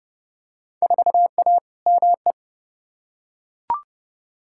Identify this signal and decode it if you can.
Morse code, heard as a keyed tone, about 30 wpm, 700 Hz. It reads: 4A ME
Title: Field Day Exchanges - 30wpm